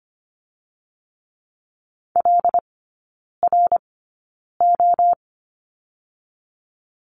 Morse code, heard as a keyed tone, about 25 wpm, spaced as for 9 wpm, 700 Hz. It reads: LRO
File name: Lesson 15e - Number 5 - Three Characters 25wpm